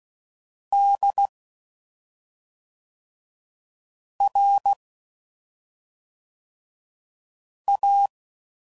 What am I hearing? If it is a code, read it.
Morse code, heard as a keyed tone, about 16 wpm, 800 Hz. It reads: D R A
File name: Suffixes / Special ING's / SlowToFaster ING's